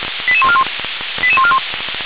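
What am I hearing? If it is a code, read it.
Five-tone selective call (ZVEI), heard as quick runs of five tones, tones of about 70 ms. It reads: 88141 88141
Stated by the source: ZVEI five-tone selective call received by SDR, fire brigade alerting.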